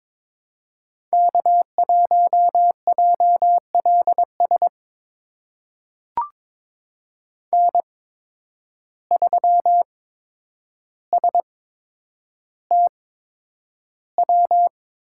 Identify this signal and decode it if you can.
Morse code, heard as a keyed tone, about 22 wpm, spaced as for 9 wpm, 700 Hz. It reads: K1JLS N3STW
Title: USA Call Signs - 5 Characters - 8x Character Spacing - 22wpm